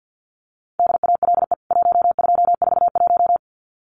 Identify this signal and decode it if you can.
Morse code, heard as a keyed tone, about 50 wpm, 700 Hz. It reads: DALE 1241
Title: CWT Exchange 50wpm